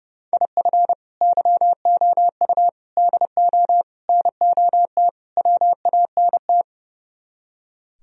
Morse code, heard as a keyed tone, about 38 wpm, 700 Hz. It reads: IF YOU DO NOT WANT